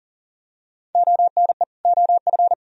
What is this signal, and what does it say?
Morse code, heard as a keyed tone, about 40 wpm, 700 Hz. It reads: ONE OF
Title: Sentences from Top 1000 Words - Review Part 1 - 40wpm